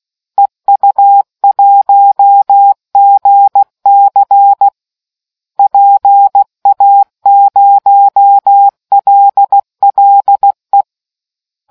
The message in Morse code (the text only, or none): EU1GC PA0LLE